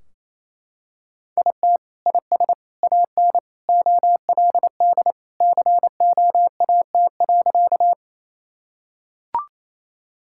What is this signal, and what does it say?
Morse code, heard as a keyed tone, about 28 wpm, 700 Hz. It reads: IT IS AN OLD COAT.
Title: Sentences from Top 1600 Words 28wpm